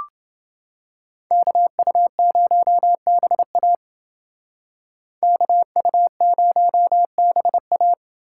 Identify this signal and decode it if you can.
Morse code, heard as a keyed tone, about 28 wpm, 700 Hz. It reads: KU0BA KU0BA